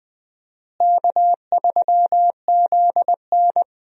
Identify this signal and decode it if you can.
Morse code, heard as a keyed tone, about 20 wpm, 700 Hz. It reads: K3ZN
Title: USA Call Signs - 4 Characters - Repeated 2x - 20wpm